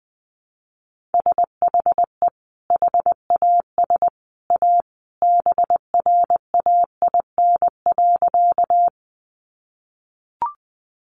Morse code, heard as a keyed tone, about 20 wpm, 700 Hz. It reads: SHE HAS A BRAIN.
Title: Sentences from Top 1100 Words 20wpm